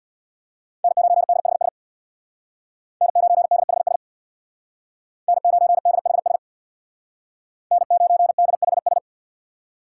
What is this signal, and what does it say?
Morse code, heard as a keyed tone, about 62 wpm, 700 Hz. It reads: N9DHS N9DHS N9DHS N9DHS
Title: Speed Racing - USA Call Signs 50wpm